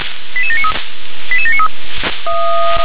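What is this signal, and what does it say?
Five-tone selective call (ZVEI), heard as quick runs of five tones, tones of about 70 ms. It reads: 88793 88793
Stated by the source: ZVEI five-tone selective call received with an SDR.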